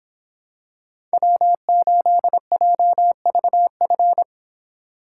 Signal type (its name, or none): Morse code